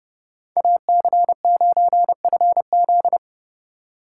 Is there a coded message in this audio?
Yes: Morse code from 0.6 to 3.2 s, heard as a keyed tone, about 40 wpm, 700 Hz. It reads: AC9FZ